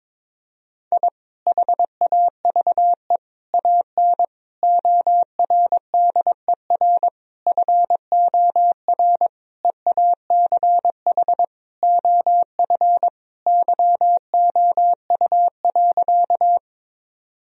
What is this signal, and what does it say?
Morse code, heard as a keyed tone, about 22 wpm, 700 Hz. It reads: I HAVE AN ORDER FOR EACH OF YOU.
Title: Sentences from Top 1000 Words - Review Part 2 - 22wpm